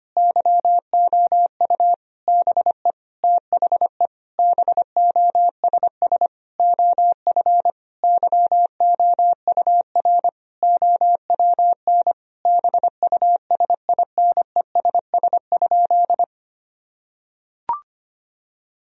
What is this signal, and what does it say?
Morse code, heard as a keyed tone, about 25 wpm, 700 Hz. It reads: YOU BE THE BOSS OF YOUR OWN BUSINESS?
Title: Sentences from Top 1000 Words - Review Part 4 - 25wpm